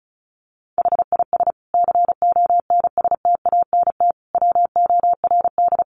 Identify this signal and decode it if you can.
Morse code, heard as a keyed tone, about 34 wpm, 700 Hz. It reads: HIS CONSTANT WORD
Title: Sentences from Top 1500 Words 35wpm